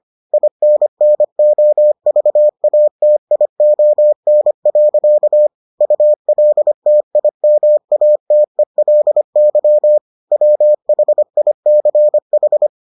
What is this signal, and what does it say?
Morse code, heard as a keyed tone, about 25 wpm, 600 Hz. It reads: INNOVATION. ULTIMATELY WHICH